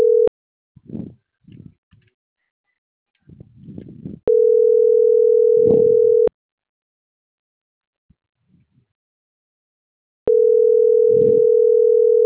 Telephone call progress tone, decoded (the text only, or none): ringback tone